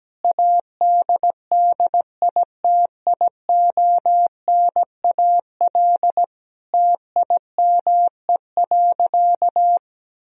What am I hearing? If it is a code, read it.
Morse code, heard as a keyed tone, about 17 wpm, 700 Hz. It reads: ADDITIONAL TIME.
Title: Sentences from Top 1000 Words - Review Part 4 - 17wpm